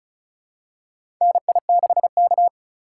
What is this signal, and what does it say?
Morse code, heard as a keyed tone, about 34 wpm, 700 Hz. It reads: NI6K
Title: USA Call Signs - 4x Character Spacing - 35wpm